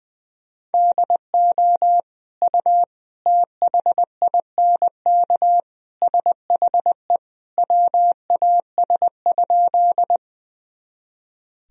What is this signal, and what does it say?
Morse code, heard as a keyed tone, about 20 wpm, 700 Hz. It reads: DO U THINK SHE WAS?
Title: Sentences from Top 200 Words and Abbreviations 20wpm